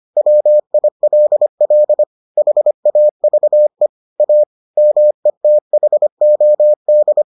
Morse code, heard as a keyed tone, about 25 wpm, 600 Hz. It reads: WILL HAVE A METHOD